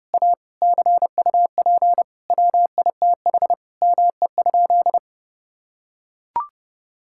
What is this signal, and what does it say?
Morse code, heard as a keyed tone, about 32 wpm, 700 Hz. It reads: A CUP WITH ME?
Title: Sentences from Top 1600 Words 30wpm